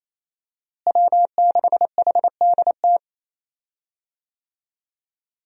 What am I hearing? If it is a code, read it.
Morse code, heard as a keyed tone, about 28 wpm, 700 Hz. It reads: W6HDT
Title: USA Call Signs - 5 Characters - Repeated 3x - 28wpm